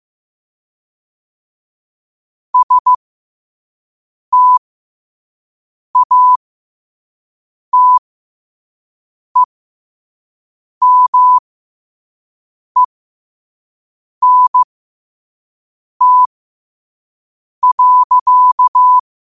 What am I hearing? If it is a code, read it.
Morse code, heard as a keyed tone, about 15 wpm, spaced as for 5 wpm, 1000 Hz. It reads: STATEMENT.
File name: qotdfiles / 5wpm